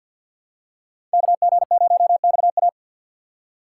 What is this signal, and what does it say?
Morse code, heard as a keyed tone, about 47 wpm, 700 Hz. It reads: KG0XA
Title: USA Call Signs - 5 Characters - 4x Character Spacing - 50wpm